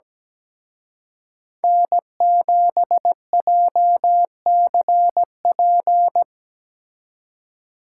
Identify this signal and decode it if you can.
Morse code, heard as a keyed tone, about 20 wpm, 700 Hz. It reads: N7JCP
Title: Speed Racing - USA 1x3 Call Signs 17wpm